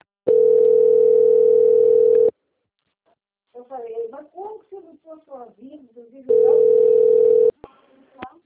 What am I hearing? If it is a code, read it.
Ringback tone (North America), 440 + 480 Hz, with speech.
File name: out- - - - - .31015